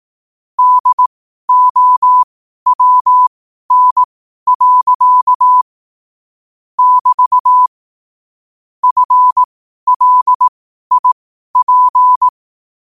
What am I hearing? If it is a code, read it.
Morse code, heard as a keyed tone, about 18 wpm, spaced as for 12 wpm, 1000 Hz. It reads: DOWN. <BT> FLIP
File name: qotdfiles / 12wpm